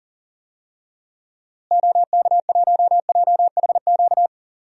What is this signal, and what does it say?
Morse code, heard as a keyed tone, about 38 wpm, 700 Hz. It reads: OK1JHQ
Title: DXCC-100 Call Signs - 6 Characters - Repeated 2x - 40wpm